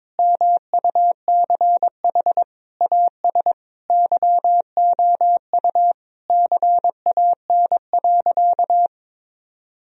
Morse code, heard as a keyed tone, about 22 wpm, 700 Hz. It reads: MUCH AS YOU CAN.
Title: Sentences from Top 1000 Words - Review Part 1 - 22wpm